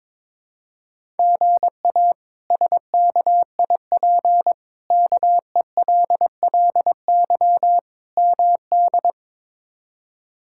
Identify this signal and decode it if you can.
Morse code, heard as a keyed tone, about 22 wpm, 700 Hz. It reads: GA SKIP KELLY MD
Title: K1USN SST QSO Practice - 22wpm